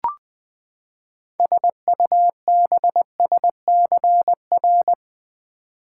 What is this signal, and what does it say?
Morse code, heard as a keyed tone, about 20 wpm, 700 Hz. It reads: SUBSCR
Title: Build A Word - 6 Letters At A Time - 12-Letter Words - 20wpm